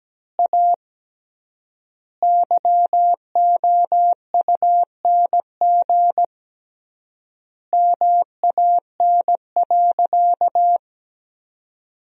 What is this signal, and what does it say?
Morse code, heard as a keyed tone, about 17 wpm, 700 Hz. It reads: A YOUNG MAN.
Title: 3x Word-Spacing - Sentences from Top 500 Words - 17wpm